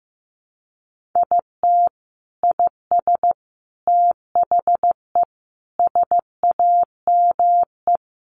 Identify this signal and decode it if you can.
Morse code, heard as a keyed tone, about 15 wpm, 700 Hz. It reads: IT IS THE SAME
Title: Sentences from Top 1100 Words 15wpm